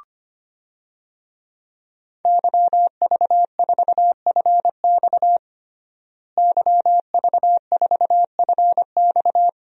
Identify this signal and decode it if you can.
Morse code, heard as a keyed tone, about 25 wpm, 700 Hz. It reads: YV4FX YV4FX
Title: DXCC-100 Call Signs - 5 Characters - Repeated 3x - 25wpm